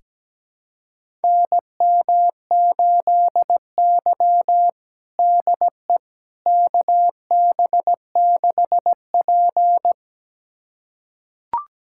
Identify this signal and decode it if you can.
Morse code, heard as a keyed tone, about 17 wpm, 700 Hz. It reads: NM8Y DE KB6P
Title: USA Callsign DE Callsign - 4 Characters - 17wpm